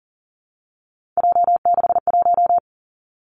Morse code, heard as a keyed tone, about 38 wpm, 700 Hz. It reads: J61
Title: Three Challenging Letter-Number Combinations 40wpm